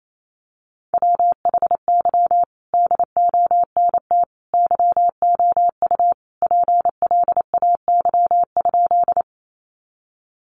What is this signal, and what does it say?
Morse code, heard as a keyed tone, about 28 wpm, 700 Hz. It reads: WHY DONT YOU PLAY?
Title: Sentences from Top 300 Words 28wpm